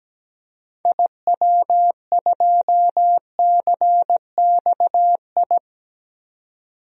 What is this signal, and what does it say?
Morse code, heard as a keyed tone, about 17 wpm, 700 Hz. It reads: IW2CXI
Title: DXCC-100 Call Signs - 6 Characters - 17wpm